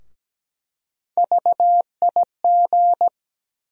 Morse code, heard as a keyed tone, about 17 wpm, 700 Hz. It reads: VIG
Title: USA Call Sign Postfixes - 3 Letters 17wpm